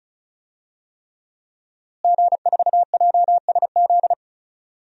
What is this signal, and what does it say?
Morse code, heard as a keyed tone, about 36 wpm, 700 Hz. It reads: G4JSZ
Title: DXCC-100 Call Signs - 5 Characters - Repeated 3x - 35wpm